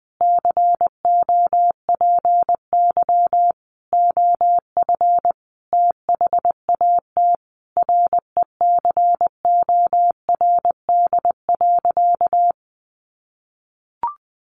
Morse code, heard as a keyed tone, about 19 wpm, 700 Hz. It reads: COPY OF THAT RECORD.
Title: Sentences from Top 1400 Words 20wpm